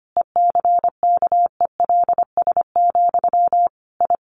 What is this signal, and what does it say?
Morse code, heard as a keyed tone, about 25 wpm, 700 Hz. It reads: ECKELS, I
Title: A Sound Of Thunder 25wpm